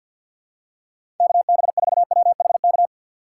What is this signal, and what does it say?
Morse code, heard as a keyed tone, about 52 wpm, 700 Hz. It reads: KB4WHK
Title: USA Call Signs - 6 Characters - 4x Character Spacing - 50wpm